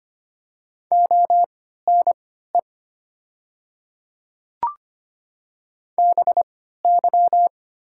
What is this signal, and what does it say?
Morse code, heard as a keyed tone, about 25 wpm, spaced as for 14 wpm, 700 Hz. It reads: ONE BY
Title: 3x Character Spacing - Top 100 Words - 25wpm